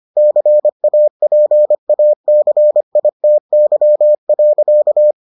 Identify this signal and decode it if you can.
Morse code, heard as a keyed tone, about 25 wpm, 600 Hz. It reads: CAPACITY.